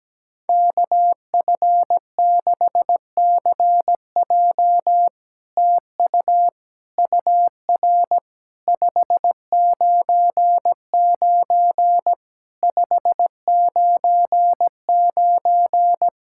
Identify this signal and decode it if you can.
Morse code, heard as a keyed tone, about 17 wpm, 700 Hz. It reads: KF6CJ TU UR 599 599
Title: SOTA QSO Practice 17wpm